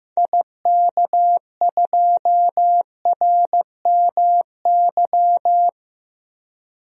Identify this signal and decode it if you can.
Morse code, heard as a keyed tone, about 15 wpm, 700 Hz. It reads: IK2RMY
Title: DXCC-100 Call Signs - 6 Characters - 15wpm